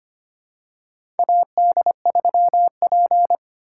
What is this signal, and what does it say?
Morse code, heard as a keyed tone, about 25 wpm, 700 Hz. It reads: AD3P
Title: USA Call Signs - 4 Characters - 4x Character Spacing - 25wpm